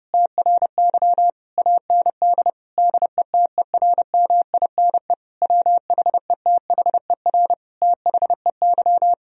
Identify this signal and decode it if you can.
Morse code, heard as a keyed tone, about 32 wpm, 700 Hz. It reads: TRY AND DETERMINE WHETHER THEY
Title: Sentences from Top 1000 Words - Review Part 4 - 30wpm